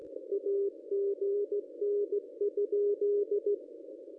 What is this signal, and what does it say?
Morse code, heard as a keyed tone, about 16 wpm, 410 Hz. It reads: AGN?
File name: contests / ss-cw-k5tr-op-n5rz -1000